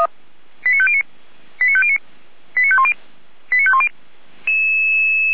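Five-tone selective call (ZVEI), heard as quick runs of five tones, tones of about 70 ms. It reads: 89509 89509 89510 89510
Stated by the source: Recorded with an SDR, ZVEI five-tone selective call.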